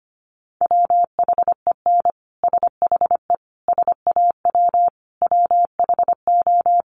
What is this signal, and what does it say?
Morse code, heard as a keyed tone, about 25 wpm, 700 Hz. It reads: WHEN SHE SAW WHO
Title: Sentences from Top 1200 Words 25wpm